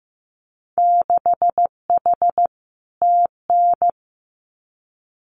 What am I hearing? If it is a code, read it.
Morse code, heard as a keyed tone, about 15 wpm, 700 Hz. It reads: 6H TN
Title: Winter Field Day Exchanges Encore 15wpm